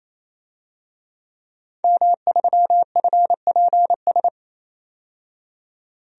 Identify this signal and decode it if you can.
Morse code, heard as a keyed tone, about 28 wpm, 700 Hz. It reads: M3FPS
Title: DXCC-100 Call Signs - 5 Characters - 28wpm